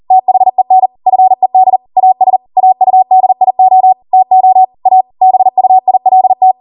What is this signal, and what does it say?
Morse code, heard as a keyed tone, about 40 wpm, 750 Hz. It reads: THEN FED AS AUDIO TO A BUILT